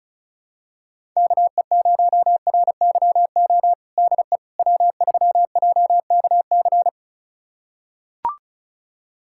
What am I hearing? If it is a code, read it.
Morse code, heard as a keyed tone, about 36 wpm, 700 Hz. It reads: KE0RYO DE W3JKC
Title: USA Callsign DE Callsign 35wpm